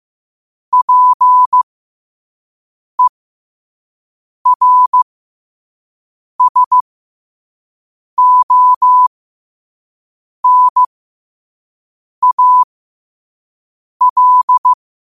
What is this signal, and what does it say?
Morse code, heard as a keyed tone, about 15 wpm, spaced as for 5 wpm, 1000 Hz. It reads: PERSONAL
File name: qotdfiles / 5wpm